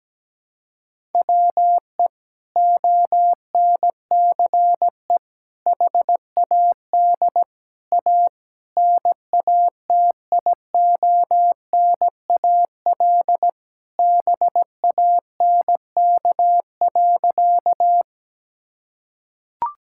Morse code, heard as a keyed tone, about 17 wpm, 700 Hz. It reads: WE ONCE HAD A NATIONAL BANK.
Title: Sentences from Top 1000 Words - Review Part 6 - 17wpm